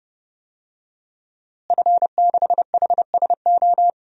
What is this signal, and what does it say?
Morse code, heard as a keyed tone, about 32 wpm, 700 Hz. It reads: F6HSO
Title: DXCC-100 Call Signs - 5 Characters - Repeated 3x - 30wpm